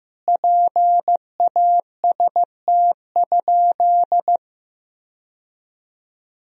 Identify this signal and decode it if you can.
Morse code, heard as a keyed tone, about 15 wpm, 700 Hz. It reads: PAST?
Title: Sentences from Top 1000 Words - Review Part 4 - 15wpm